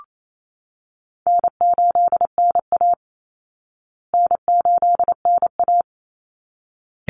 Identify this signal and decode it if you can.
Morse code, heard as a keyed tone, about 28 wpm, 700 Hz. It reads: N8NA N8NA
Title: Easy Call Signs Repeated 2x - 28wpm